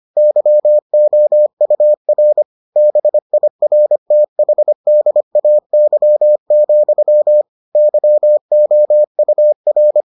Morse code, heard as a keyed tone, about 25 wpm, 600 Hz. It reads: YOUR BIRTHDAY, YOUR